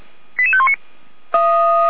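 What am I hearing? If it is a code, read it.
Five-tone selective call (ZVEI), heard as quick runs of five tones, tones of about 70 ms. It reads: 88519
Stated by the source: Five-tone ZVEI selective call received on an SDR.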